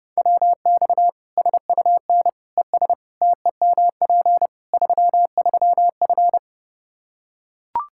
Morse code, heard as a keyed tone, about 30 wpm, 700 Hz. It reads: WX SUN ES TEMP 33F
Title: K1USN SST QSO Practice 30wpm